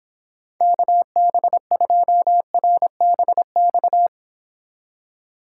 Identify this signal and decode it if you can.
Morse code, heard as a keyed tone, about 31 wpm, 700 Hz. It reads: KB2RBX